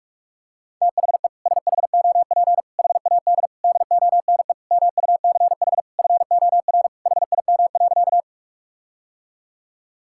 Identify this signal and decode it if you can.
Morse code, heard as a keyed tone, about 46 wpm, 700 Hz. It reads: THE SHOP HAD DONE MUCH FOR HIM.
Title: Sentences from Top 1600 Words 45wpm